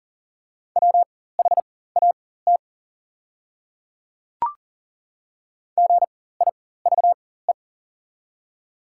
Morse code, heard as a keyed tone, about 39 wpm, spaced as for 19 wpm, 690 Hz. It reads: WHAT GIVE
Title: 4x Character Spacing - Top 300 Words - 40wpm